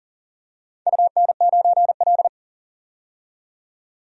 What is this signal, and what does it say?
Morse code, heard as a keyed tone, about 37 wpm, 700 Hz. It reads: UN9L
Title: DXCC-100 Call Signs - 4 Characters - 40wpm